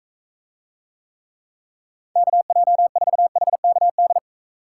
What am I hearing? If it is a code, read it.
Morse code, heard as a keyed tone, about 44 wpm, 700 Hz. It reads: KJ4HKD